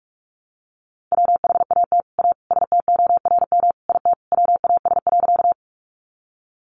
Morse code, heard as a keyed tone, about 44 wpm, 700 Hz. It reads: WHAT A STORM IT WAS.
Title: Sentences from Top 1500 Words 45wpm